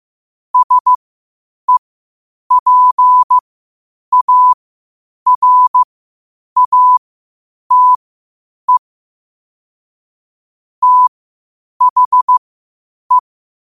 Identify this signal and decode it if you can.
Morse code, heard as a keyed tone, about 15 wpm, spaced as for 8 wpm, 1000 Hz. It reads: SEPARATE THE
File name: qotdfiles / 8wpm